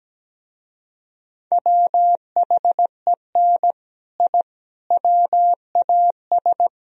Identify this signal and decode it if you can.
Morse code, heard as a keyed tone, about 18 wpm, 700 Hz. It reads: WHEN I WAS
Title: Speed-Racing - Sets of 3 Words 17wpm